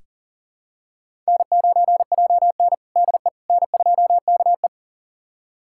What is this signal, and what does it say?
Morse code, heard as a keyed tone, about 40 wpm, 700 Hz. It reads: N9JN DE N2KE
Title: USA Callsign DE Callsign - 4 Characters - 40wpm